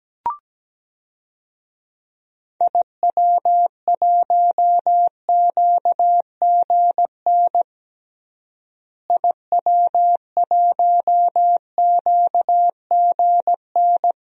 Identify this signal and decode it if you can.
Morse code, heard as a keyed tone, about 17 wpm, 700 Hz. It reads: IW1QGN IW1QGN
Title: DXCC-100 Call Signs - 6 Characters - Repeated 2x - 17wpm